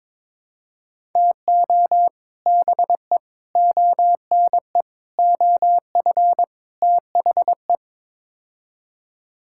Morse code, heard as a keyed tone, about 22 wpm, 700 Hz. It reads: TO BE ONE OF THE
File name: practice / 22wpm / Build A Phrase - 5-Word Phrases - 22wpm